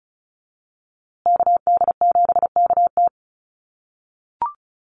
Morse code, heard as a keyed tone, about 34 wpm, 700 Hz. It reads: KD7KT
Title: USA Call Signs Repeated 2x 35wpm